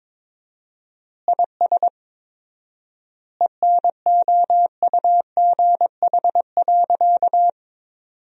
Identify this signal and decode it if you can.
Morse code, heard as a keyed tone, about 22 wpm, 690 Hz. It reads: IS ENOUGH.
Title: 4x Word-Spacing - Sentences from Top 400 Words - 22wpm